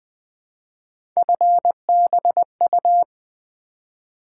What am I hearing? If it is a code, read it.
Morse code, heard as a keyed tone, about 20 wpm, 700 Hz. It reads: FBU
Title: Three Letter-Number Combinations - 20wpm